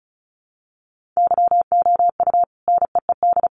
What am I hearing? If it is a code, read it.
Morse code, heard as a keyed tone, about 34 wpm, 700 Hz. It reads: YOU NEED
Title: Sentences from Top 1500 Words 35wpm